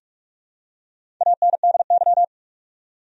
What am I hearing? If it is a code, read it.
Morse code, heard as a keyed tone, about 47 wpm, 700 Hz. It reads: ANDY
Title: USA Common Names - Max 5 Letters - 45wpm